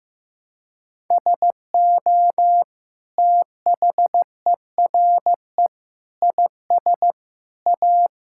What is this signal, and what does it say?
Morse code, heard as a keyed tone, about 16 wpm, 700 Hz. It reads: SO THERE IS A
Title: Sentences from Top 1600 Words 15wpm